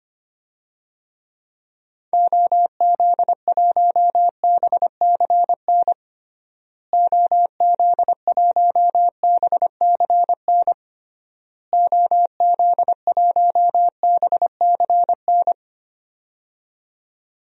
Morse code, heard as a keyed tone, about 25 wpm, 700 Hz. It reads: OZ1BCN OZ1BCN OZ1BCN
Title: DXCC-100 Call Signs - 6 Characters - Repeated 3x - 25wpm